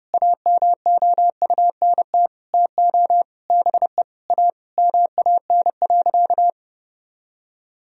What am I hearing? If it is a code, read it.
Morse code, heard as a keyed tone, about 30 wpm, 700 Hz. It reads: AMOUNT TO BE A MAN.
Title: Sentences from Top 1000 Words - Review Part 4 - 30wpm